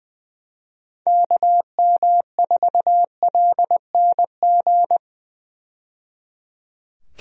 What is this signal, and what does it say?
Morse code, heard as a keyed tone, about 21 wpm, 700 Hz. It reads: KM4LNG
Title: USA Call Signs - 6 Characters - 20wpm